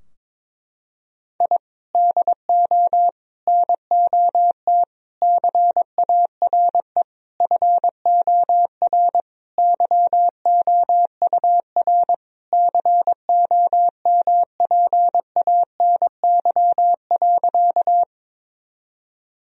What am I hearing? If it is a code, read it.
Morse code, heard as a keyed tone, about 23 wpm, 700 Hz. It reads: I DO NOT CARE FOR YOUR COMPANY.
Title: Sentences from Top 1000 Words - Review Part 3 - 22wpm